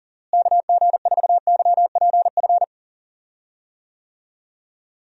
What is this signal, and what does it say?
Morse code, heard as a keyed tone, about 40 wpm, 700 Hz. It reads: KG4YPF
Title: USA Call Signs - 6 Characters - Repeated 3x - 40wpm